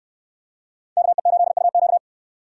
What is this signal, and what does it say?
Morse code, heard as a keyed tone, about 68 wpm, 700 Hz. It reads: KE8UQ